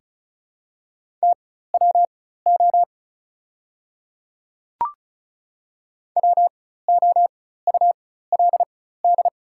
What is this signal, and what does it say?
Morse code, heard as a keyed tone, about 36 wpm, spaced as for 16 wpm, 700 Hz. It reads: TWO WOULD
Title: 4x Character Spacing - Top 200 Words - 35wpm